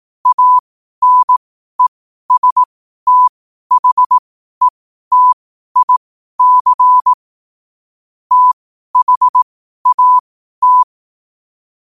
Morse code, heard as a keyed tone, about 18 wpm, spaced as for 12 wpm, 1000 Hz. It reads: ANESTHETIC THAT
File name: qotdfiles / 12wpm